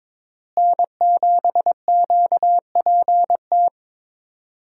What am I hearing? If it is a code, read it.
Morse code, heard as a keyed tone, about 22 wpm, 700 Hz. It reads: N7QPT
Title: USA Call Signs - 5 Characters - Repeated 3x - 22wpm